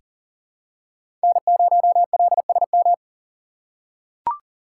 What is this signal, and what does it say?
Morse code, heard as a keyed tone, about 40 wpm, 700 Hz. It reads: N0LSM
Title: USA Call Signs - 6x Character Spacing - 40wpm